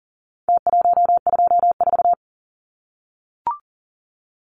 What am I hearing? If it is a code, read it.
Morse code, heard as a keyed tone, about 40 wpm, 700 Hz. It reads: T124
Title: Numbers - Contest Style - 4-Digit 40wpm